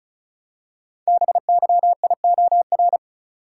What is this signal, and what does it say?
Morse code, heard as a keyed tone, about 36 wpm, 700 Hz. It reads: BYIOR
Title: Code Groups - 5 Letters - 35wpm